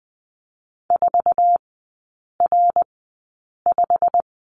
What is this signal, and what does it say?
Morse code, heard as a keyed tone, about 20 wpm, spaced as for 8 wpm, 690 Hz. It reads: 4R5
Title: 205 Lesson 33e - Letter j - Three Characters 20wpm